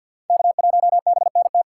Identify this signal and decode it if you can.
Morse code, heard as a keyed tone, about 51 wpm, 700 Hz. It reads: K1BNT
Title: USA Call Signs - 5 Characters - 4x Character Spacing - 50wpm